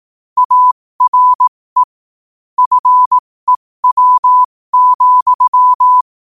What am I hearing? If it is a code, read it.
Morse code, heard as a keyed tone, about 18 wpm, spaced as for 15 wpm, 1000 Hz. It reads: ARE FEW,
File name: qotdfiles / 15wpm